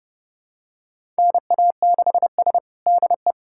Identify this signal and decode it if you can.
Morse code, heard as a keyed tone, about 30 wpm, 700 Hz. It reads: NA6S DE
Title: USA Callsign DE Callsign - 4 Characters - 30wpm